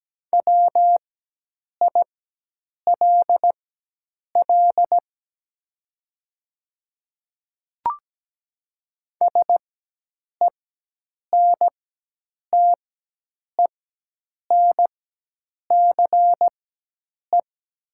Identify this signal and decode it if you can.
Morse code, heard as a keyed tone, about 17 wpm, spaced as for 8 wpm, 700 Hz. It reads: WILL SENTENCE
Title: 4x Character Spacing - Top 300 Words - 17wpm